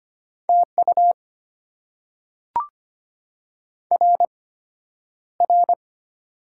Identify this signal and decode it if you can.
Morse code, heard as a keyed tone, about 25 wpm, spaced as for 11 wpm, 690 Hz. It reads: TU RR
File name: Top 90 QSO Elements - 8x Character Spacing - 25wpm